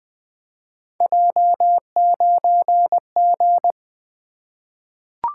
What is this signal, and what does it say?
Morse code, heard as a keyed tone, about 20 wpm, 700 Hz. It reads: J9G
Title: Three Letter-Number Combinations - 20wpm